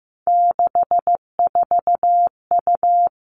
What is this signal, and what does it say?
Morse code, heard as a keyed tone, about 15 wpm, 700 Hz. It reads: 64U
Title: Three Challenging Letter-Number Combinations 15wpm